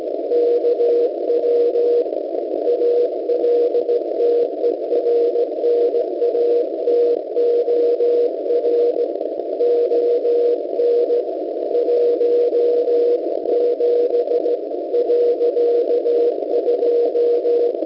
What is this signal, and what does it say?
Morse code, heard as a keyed tone, about 16 wpm, 480 Hz. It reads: KW ALTERNATOR ON 17.2